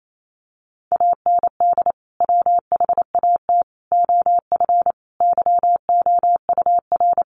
Morse code, heard as a keyed tone, about 28 wpm, 700 Hz. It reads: AND WHAT OF YOUR
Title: Sentences from Top 900 Words 28wpm